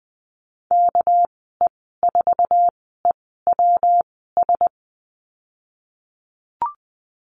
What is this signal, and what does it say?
Morse code, heard as a keyed tone, about 20 wpm, spaced as for 15 wpm, 700 Hz. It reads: KE4EWS
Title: 200 Lesson 32g - Letter k - Call Signs 20wpm